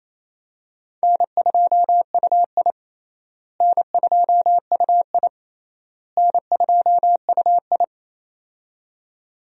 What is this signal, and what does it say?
Morse code, heard as a keyed tone, about 28 wpm, 700 Hz. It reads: N2UI N2UI N2UI
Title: USA Call Signs - 4 Characters - Repeated 3x - 28wpm